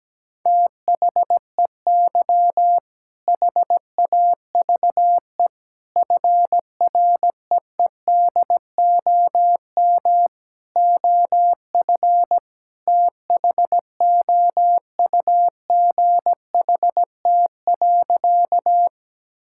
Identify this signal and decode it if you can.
Morse code, heard as a keyed tone, about 17 wpm, 700 Hz. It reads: THEY HAVE FREEDOM OF THOUGHT.
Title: Sentences from Top 1000 Words - Review Part 5 - 17wpm